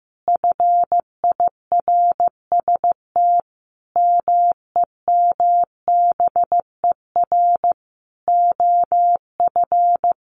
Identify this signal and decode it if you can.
Morse code, heard as a keyed tone, about 15 wpm, 700 Hz. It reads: FIRST MEMBER OF
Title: Sentences from Top 1400 Words 15wpm